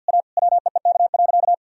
Morse code, heard as a keyed tone, about 49 wpm, 700 Hz. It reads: A WEEK.
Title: Sentences from Top 1000 Words - Review Part 2 - 50wpm